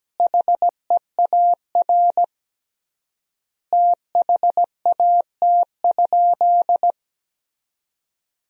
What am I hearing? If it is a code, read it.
Morse code, heard as a keyed tone, about 17 wpm, 700 Hz. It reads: HEAR THAT?
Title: 3x Word-Spacing - Sentences from Top 300 Words - 17wpm